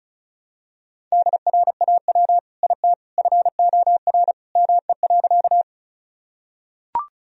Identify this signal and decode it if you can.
Morse code, heard as a keyed tone, about 34 wpm, 700 Hz. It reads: DRAW IT FOR ME.
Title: Sentences from Top 1000 Words - Review Part 1 - 35wpm